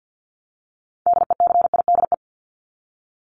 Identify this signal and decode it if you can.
Morse code, heard as a keyed tone, about 50 wpm, 700 Hz. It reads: DECIDE
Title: ED Suffix - Top 100 Words with Base Words 50wpm